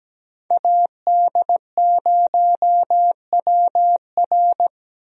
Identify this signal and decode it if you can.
Morse code, heard as a keyed tone, about 17 wpm, 700 Hz. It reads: AD0WR
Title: USA Call Signs - 5 Characters - Repeated 3x - 17wpm